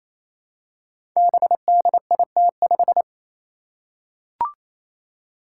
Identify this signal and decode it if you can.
Morse code, heard as a keyed tone, about 27 wpm, 700 Hz. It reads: BDIT5
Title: Five Challenging Letter-Number Combinations 28wpm